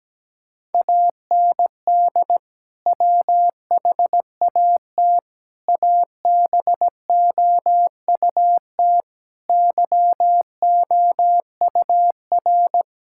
Morse code, heard as a keyed tone, about 17 wpm, 700 Hz. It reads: AND WHAT ABOUT YOUR
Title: Sentences from Top 1000 Words - Review Part 6 - 17wpm